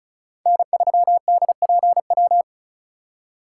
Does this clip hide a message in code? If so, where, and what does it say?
Yes: Morse code from 0.5 to 2.4 s, heard as a keyed tone, about 45 wpm, 700 Hz. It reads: N3DPW